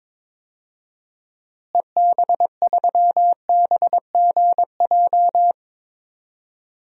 Morse code, heard as a keyed tone, about 22 wpm, 700 Hz. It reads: EB3BGJ
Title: DXCC-100 Call Signs - 6 Characters - Repeated 3x - 22wpm